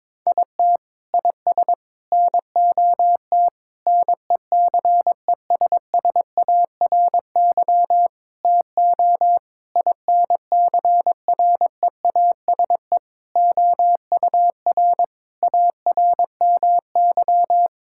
Morse code, heard as a keyed tone, about 22 wpm, 700 Hz. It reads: IT IS NOT NECESSARY TO INCREASE OUR ARMY
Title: Sentences from Top 1000 Words - Review Part 6 - 22wpm